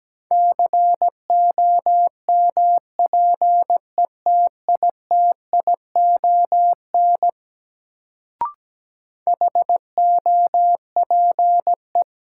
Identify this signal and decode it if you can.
Morse code, heard as a keyed tone, about 17 wpm, 700 Hz. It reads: COMPETITION HOPE
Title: Sentences from Top 1000 Words - New Word Drill - 17wpm